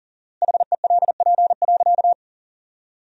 Morse code, heard as a keyed tone, about 37 wpm, 700 Hz. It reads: HELP.
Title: 4x Word-Spacing - Sentences from Top 200 Words - 40wpm